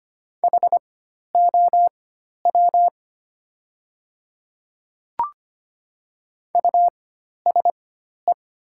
Morse code, heard as a keyed tone, about 25 wpm, spaced as for 12 wpm, 700 Hz. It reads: HOW USE
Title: 4x Character Spacing - Top 100 Words - 25wpm